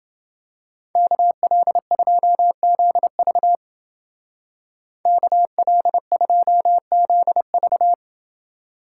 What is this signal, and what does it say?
Morse code, heard as a keyed tone, about 28 wpm, 700 Hz. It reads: KL2ZV KL2ZV